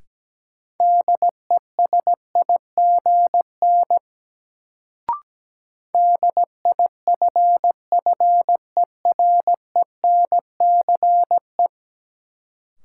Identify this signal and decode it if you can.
Morse code, heard as a keyed tone, about 17 wpm, 700 Hz. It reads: DESIGN DIFFERENCE
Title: Sentences from Top 800 Words - New Word Drill - 17wpm